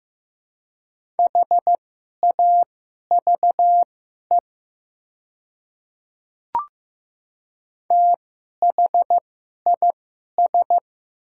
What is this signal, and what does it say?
Morse code, heard as a keyed tone, about 15 wpm, spaced as for 11 wpm, 700 Hz. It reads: HAVE THIS
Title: 2x Character Spacing - Top 500 Words - 15wpm